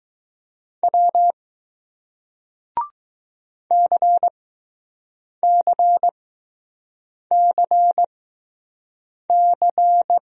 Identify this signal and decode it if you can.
Morse code, heard as a keyed tone, about 20 wpm, spaced as for 6 wpm, 700 Hz. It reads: W CCCC